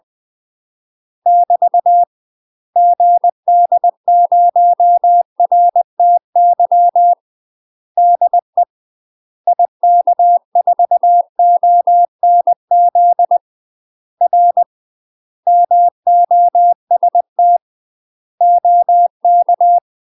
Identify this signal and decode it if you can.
Morse code, heard as a keyed tone, about 20 wpm, 700 Hz. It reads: <BT> GD0RTY DE IK4ONZ R MOST OK
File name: UK MASTER QSOs TRAINING / 20WPM / UK MASTER QSOs 0005